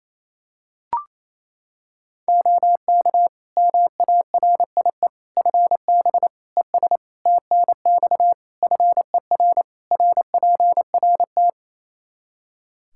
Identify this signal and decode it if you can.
Morse code, heard as a keyed tone, about 28 wpm, 700 Hz. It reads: OK MARIE FB ES TNX FER RPRT
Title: Practice QSO - Exchange 2 - Info - 28wpm